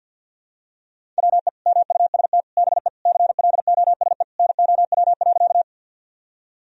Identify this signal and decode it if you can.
Morse code, heard as a keyed tone, about 53 wpm, 700 Hz. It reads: WE MUST BE CLOSE NOW.
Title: Sentences from Top 1000 Words - Review Part 1 - 50wpm